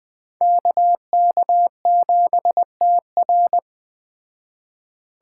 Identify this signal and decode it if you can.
Morse code, heard as a keyed tone, about 20 wpm, 700 Hz. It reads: KK7TR
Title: Speed Racing - USA 2x2 Call Signs 17wpm